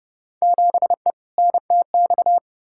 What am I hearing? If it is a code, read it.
Morse code, heard as a keyed tone, about 31 wpm, 700 Hz. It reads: 7E NTX
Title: Field Day Exchanges Encore - 30wpm